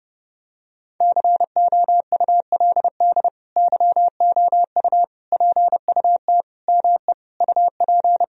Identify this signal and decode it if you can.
Morse code, heard as a keyed tone, about 32 wpm, 700 Hz. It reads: COULD YOU PUT ME UP
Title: Sentences from Top 1000 Words - Review Part 3 - 30wpm